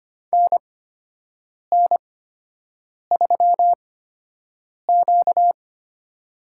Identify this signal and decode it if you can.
Morse code, heard as a keyed tone, about 25 wpm, spaced as for 7 wpm, 700 Hz. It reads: NN3Q
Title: USA Call Signs - 4 Characters - 8x Character Spacing - 25wpm